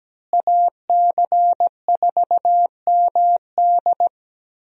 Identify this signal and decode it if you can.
Morse code, heard as a keyed tone, about 17 wpm, 700 Hz. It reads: AC4MD
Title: USA Call Signs - 5 Characters - Repeated 3x - 17wpm